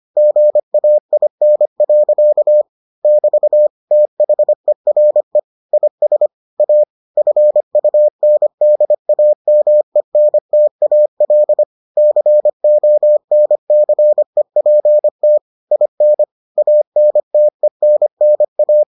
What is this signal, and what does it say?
Morse code, heard as a keyed tone, about 25 wpm, 600 Hz. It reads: GAIN. <BT> THERE IS A FUNDAMENTAL CONCEPT IN ANTENNA